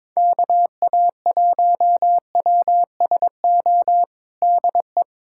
Morse code, heard as a keyed tone, about 22 wpm, 700 Hz. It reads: KA1WSO DE
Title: USA Callsign DE Callsign - 6 Characters - 22wpm